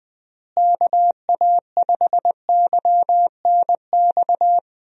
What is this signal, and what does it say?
Morse code, heard as a keyed tone, about 20 wpm, 700 Hz. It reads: KA5YNX
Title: USA Call Signs - 6 Characters - Repeated 2x - 20wpm